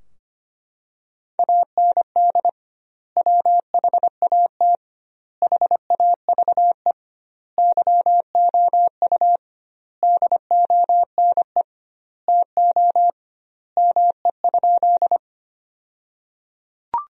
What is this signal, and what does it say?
Morse code, heard as a keyed tone, about 25 wpm, 700 Hz. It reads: AND WHAT HAVE YOU DONE TO ME?
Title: 2x Word-Spacing - Sentences from Top 500 Words - 25wpm